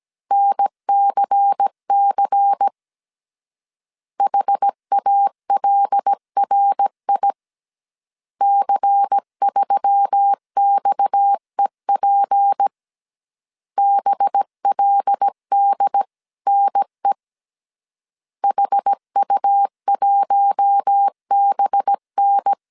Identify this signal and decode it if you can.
Morse code, heard as a keyed tone, about 17 wpm, 800 Hz. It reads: NCC HALRI C3XEP BLDNE HU1BN
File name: CW / DET / 1 6 1500C S A 16wpm 2635